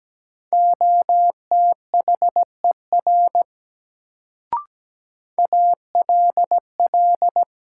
Morse code, heard as a keyed tone, about 17 wpm, 700 Hz. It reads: OTHER ALL
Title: Sentences from Top 100 Words - New Word Drill - 17wpm